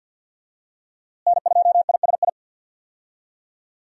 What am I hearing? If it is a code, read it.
Morse code, heard as a keyed tone, about 51 wpm, 700 Hz. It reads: N2ISI
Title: USA Call Signs - 5 Characters - Repeated 3x - 50wpm